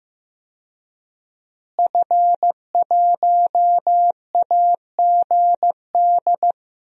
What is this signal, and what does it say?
Morse code, heard as a keyed tone, about 15 wpm, 700 Hz. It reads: F1AGD
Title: DXCC-100 Call Signs - 5 Characters - Repeated 3x - 15wpm